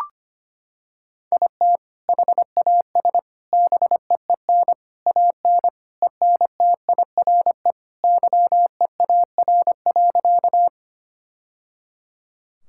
Morse code, heard as a keyed tone, about 25 wpm, 700 Hz. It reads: IT HAS BEEN AN ENTIRE YEAR.
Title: Sentences from Top 1000 Words - Review Part 5 - 25wpm